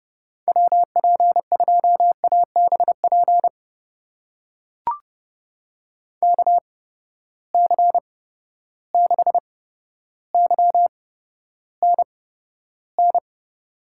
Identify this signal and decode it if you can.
Morse code, heard as a keyed tone, about 31 wpm, spaced as for 13 wpm, 700 Hz. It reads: WP2ABP KC6YNN
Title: USA Call Signs - 6 Characters - 8x Character Spacing - 30wpm